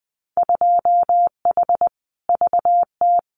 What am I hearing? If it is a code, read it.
Morse code, heard as a keyed tone, about 20 wpm, 700 Hz. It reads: 2H VT
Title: Winter Field Day Exchanges 20wpm